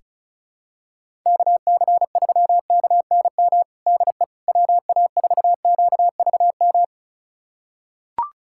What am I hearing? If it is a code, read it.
Morse code, heard as a keyed tone, about 36 wpm, 700 Hz. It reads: KC3KNM DE WA4QVM
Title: USA Callsign DE Callsign - 6 Characters - 35wpm